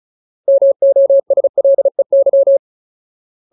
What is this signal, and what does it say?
Morse code, heard as a keyed tone, about 34 wpm, 550 Hz. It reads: MOSLEY
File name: Ultimate Word List / 35WPM Ultimate